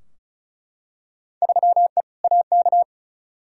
Morse code, heard as a keyed tone, about 36 wpm, 700 Hz. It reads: 3E AK
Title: Field Day Exchanges Encore - 35wpm